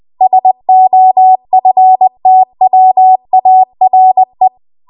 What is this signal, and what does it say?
Morse code, heard as a keyed tone, about 20 wpm, 750 Hz. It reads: SOFTWARE
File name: Archive / 20 WPM / 20WPM